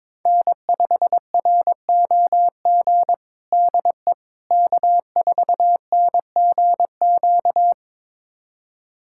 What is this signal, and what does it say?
Morse code, heard as a keyed tone, about 22 wpm, 700 Hz. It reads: N5ROG DE K4NGQ
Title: USA Callsign DE Callsign - 5 Characters - 22wpm